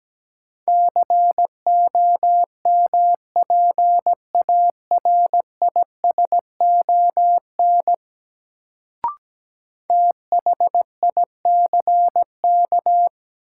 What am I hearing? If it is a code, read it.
Morse code, heard as a keyed tone, about 17 wpm, 700 Hz. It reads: COMPARISON THICK
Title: Sentences from Top 1600 Words - New Word Drill - 17wpm